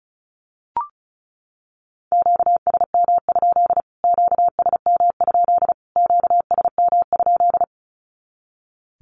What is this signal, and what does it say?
Morse code, heard as a keyed tone, about 34 wpm, 700 Hz. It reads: QSM? QSM? QSM?
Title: Q-Codes Repeated 3x 35wpm